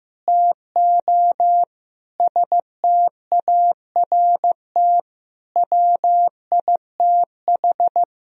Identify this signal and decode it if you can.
Morse code, heard as a keyed tone, about 15 wpm, 700 Hz. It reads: TO START WITH
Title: Sentences from Top 1000 Words - Review Part 1 - 15wpm